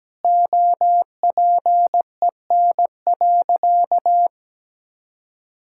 Morse code, heard as a keyed tone, about 17 wpm, 700 Hz. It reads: OPEN.
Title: Sentences from Top 1000 Words - Review Part 6 - 17wpm